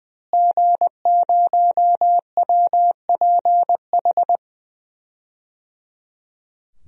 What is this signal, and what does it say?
Morse code, heard as a keyed tone, about 21 wpm, 700 Hz. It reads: G0WPH
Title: DXCC-100 Call Signs - 5 Characters - Repeated 3x - 20wpm